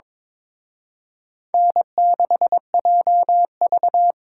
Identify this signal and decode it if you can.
Morse code, heard as a keyed tone, about 26 wpm, 700 Hz. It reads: N6JV
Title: Speed Racing - USA 1x2 Call Signs 22wpm